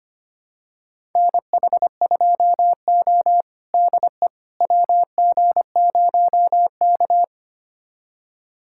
Morse code, heard as a keyed tone, about 25 wpm, 700 Hz. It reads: NH2O DE WG0K
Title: USA Callsign DE Callsign - 4 Characters - 25wpm